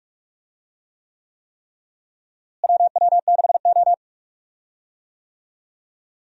Morse code, heard as a keyed tone, about 43 wpm, 700 Hz. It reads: WW6O
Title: USA Call Signs - 4 Characters - Repeated 3x - 45wpm